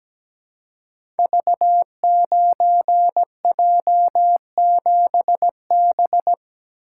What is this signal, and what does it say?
Morse code, heard as a keyed tone, about 17 wpm, 700 Hz. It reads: V9J7B
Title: Code Groups - 5 Letter:Numbers - 17wpm